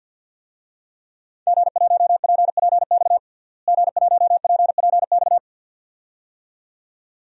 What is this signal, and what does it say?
Morse code, heard as a keyed tone, about 47 wpm, 700 Hz. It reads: G1PPX G1PPX
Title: DXCC-100 Call Signs - 5 Characters - Repeated 2x - 50wpm